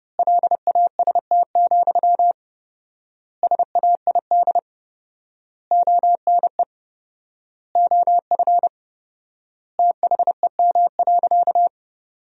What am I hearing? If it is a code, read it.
Morse code, heard as a keyed tone, about 30 wpm, 700 Hz. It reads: LAST, SAID ONE OF THEM.
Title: 4x Word-Spacing - Sentences from Top 300 Words - 30wpm